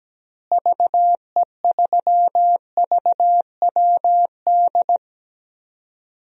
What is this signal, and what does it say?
Morse code, heard as a keyed tone, about 17 wpm, 700 Hz. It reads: VE3VWD
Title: DXCC-100 Call Signs - 6 Characters - Repeated 3x - 17wpm